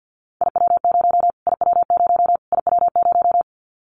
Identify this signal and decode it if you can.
Morse code, heard as a keyed tone, about 50 wpm, 700 Hz. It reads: IW0 IW0 IW0
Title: Callsign Prefixes for Top 25 DXCC - Repeated 3x - 50wpm